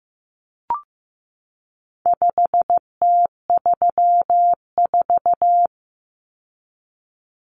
Morse code, heard as a keyed tone, about 15 wpm, 700 Hz. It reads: 5T34
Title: Common Cut Numbers - 4-Digit 15wpm